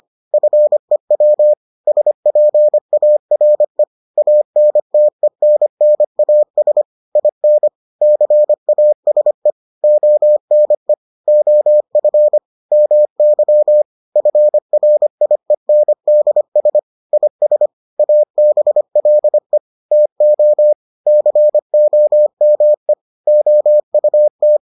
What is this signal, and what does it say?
Morse code, heard as a keyed tone, about 25 wpm, 600 Hz. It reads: FEW SPARE ANTENNAS IN CASE ONE OF MY FRIENDS IS ABLE TO COME OUT